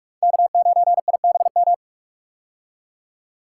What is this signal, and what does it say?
Morse code, heard as a keyed tone, about 42 wpm, 700 Hz. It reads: K9IBM
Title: K1USN SST QSO Practice - 45wpm